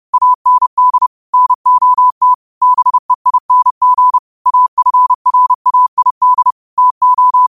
Morse code, heard as a keyed tone, about 30 wpm, 1000 Hz. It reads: AND NOT BEING AFRAID TO
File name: qotdfiles / 30wpm